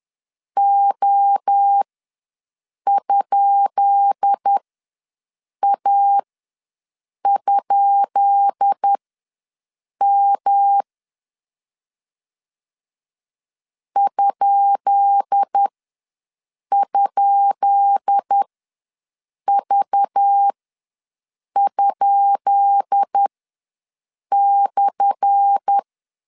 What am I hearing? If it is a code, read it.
Morse code, heard as a keyed tone, about 10 wpm, spaced as for 6 wpm, 800 Hz. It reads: O?A?M ??V?/
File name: LEZIONI / 10 SIGNS / 10B INT ws 10wpm 4134